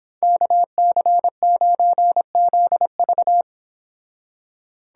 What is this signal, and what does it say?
Morse code, heard as a keyed tone, about 26 wpm, 700 Hz. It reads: KC9ZV